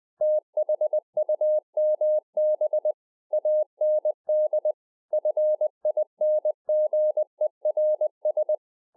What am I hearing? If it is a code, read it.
Morse code, heard as a keyed tone, about 20 wpm, 610 Hz. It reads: THUMB AND FINGERS